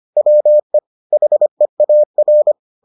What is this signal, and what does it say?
Morse code, heard as a keyed tone, about 25 wpm, 600 Hz. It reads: WE HEAR